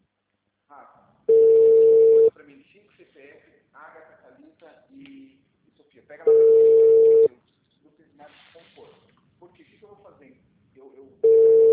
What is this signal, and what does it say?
Ringback tone (China, 450 Hz), with speech.
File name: out- -2033- - - .84